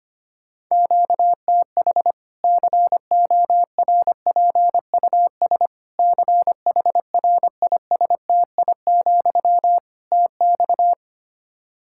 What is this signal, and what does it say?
Morse code, heard as a keyed tone, about 25 wpm, 700 Hz. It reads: QTH CORPUS CHRISTI, TX
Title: Practice QSO - Exchange 1 - Beginning - 25wpm